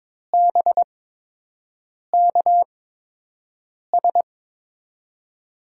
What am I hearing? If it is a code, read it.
Morse code, heard as a keyed tone, about 22 wpm, spaced as for 6 wpm, 700 Hz. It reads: BKS
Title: USA Call Signs - 8x Character Spacing - 22wpm